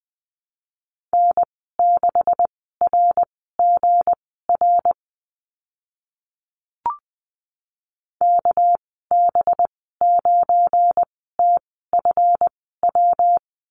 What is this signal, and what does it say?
Morse code, heard as a keyed tone, about 20 wpm, spaced as for 15 wpm, 700 Hz. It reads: N6RGR KB9TFW
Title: 200 Lesson 32g - Letter k - Call Signs 20wpm